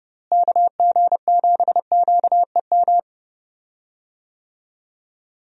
Morse code, heard as a keyed tone, about 30 wpm, 700 Hz. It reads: KG7QEM
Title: USA Call Signs - 6 Characters - Repeated 3x - 30wpm